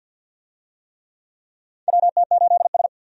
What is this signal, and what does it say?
Morse code, heard as a keyed tone, about 50 wpm, 700 Hz. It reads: WT8S
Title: USA Call Signs - 4 Characters - 50wpm